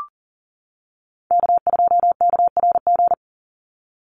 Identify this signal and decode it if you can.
Morse code, heard as a keyed tone, about 38 wpm, 700 Hz. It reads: K2KRG
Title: Call Signs - 40wpm